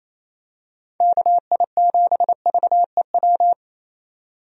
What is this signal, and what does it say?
Morse code, heard as a keyed tone, about 28 wpm, 700 Hz. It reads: KI7VEW
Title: USA Call Signs - 6 Characters - 28wpm